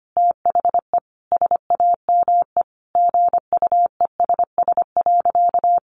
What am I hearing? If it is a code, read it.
Morse code, heard as a keyed tone, about 25 wpm, 700 Hz. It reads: THE SAME GUESS.
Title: Sentences from Top 1500 Words 25wpm